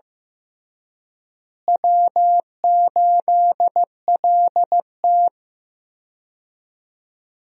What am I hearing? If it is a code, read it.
Morse code, heard as a keyed tone, about 15 wpm, 700 Hz. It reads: W8LT
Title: USA Call Signs - 4 Characters - Repeated 3x - 15wpm